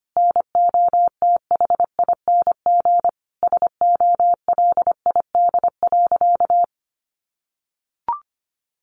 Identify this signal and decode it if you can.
Morse code, heard as a keyed tone, about 25 wpm, 700 Hz. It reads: NOTHING SOLID.
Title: Sentences from Top 1500 Words 25wpm